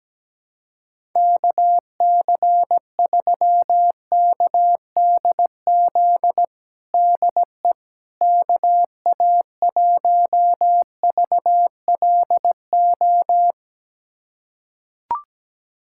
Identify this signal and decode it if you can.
Morse code, heard as a keyed tone, about 17 wpm, 700 Hz. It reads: KC3KDZ DE KA1VLO
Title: USA Callsign DE Callsign - 6 Characters - 17wpm